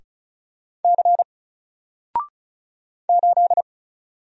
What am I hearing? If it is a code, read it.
Morse code, heard as a keyed tone, about 36 wpm, 700 Hz. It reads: C 8
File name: Single-Letter-Number-35wpm